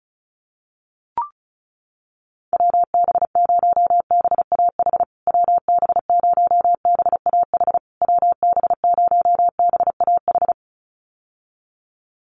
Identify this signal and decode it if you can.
Morse code, heard as a keyed tone, about 34 wpm, 700 Hz. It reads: WB0BAH WB0BAH WB0BAH
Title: USA Call Signs Repeated 3x 35wpm